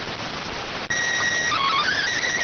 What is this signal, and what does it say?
SSTV (slow-scan television), PD 290.